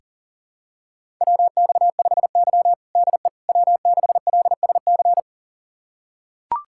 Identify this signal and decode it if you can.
Morse code, heard as a keyed tone, about 40 wpm, 700 Hz. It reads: WX5Y DE W6LSC
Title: USA Callsign DE Callsign 40wpm